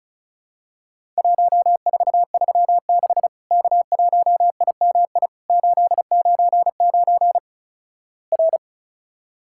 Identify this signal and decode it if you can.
Morse code, heard as a keyed tone, about 36 wpm, 700 Hz. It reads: 1436 K1IMI 899 R
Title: Worked All Europe DX Contest - QTC Exchange - 35wpm